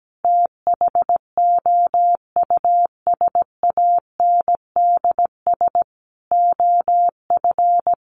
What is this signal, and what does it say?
Morse code, heard as a keyed tone, about 17 wpm, 700 Hz. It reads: THOUSANDS OF
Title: Sets of 5 Words Encore2 17wpm